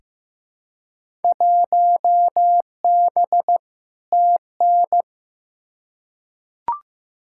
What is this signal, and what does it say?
Morse code, heard as a keyed tone, about 15 wpm, 700 Hz. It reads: 1B TN
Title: Field Day Exchanges - 15wpm